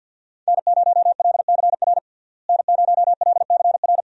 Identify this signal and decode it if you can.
Morse code, heard as a keyed tone, about 48 wpm, 700 Hz. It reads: N0LCR N0LCR
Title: USA Call Signs - 5 Characters - Repeated 2x - 50wpm